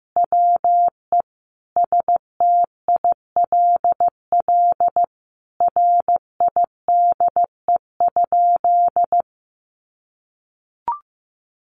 Morse code, heard as a keyed tone, about 15 wpm, 700 Hz. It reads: WE STILL RIDE?
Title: Sentences from Top 1200 Words 15wpm